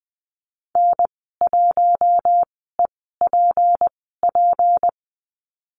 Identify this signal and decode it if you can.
Morse code, heard as a keyed tone, about 20 wpm, spaced as for 15 wpm, 700 Hz. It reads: N1EPP
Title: 154 Lesson 25g - Letter p - Call Signs 20wpm